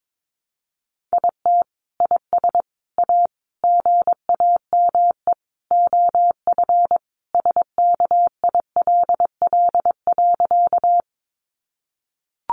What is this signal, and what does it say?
Morse code, heard as a keyed tone, about 22 wpm, 700 Hz. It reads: IT IS A GAME OF SKILL.
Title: Sentences from Top 1200 Words 22wpm